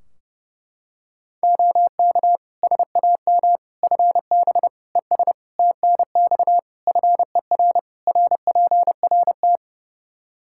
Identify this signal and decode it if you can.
Morse code, heard as a keyed tone, about 30 wpm, 700 Hz. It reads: OK SAM FB ES TNX FER RPRT
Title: Practice QSO - Exchange 2 - Info - 30wpm